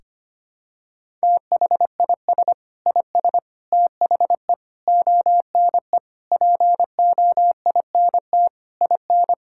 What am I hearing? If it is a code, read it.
Morse code, heard as a keyed tone, about 25 wpm, 700 Hz. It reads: THIS IS THE ONE POINT IN
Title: Sentences from Top 1000 Words - Review Part 5 - 25wpm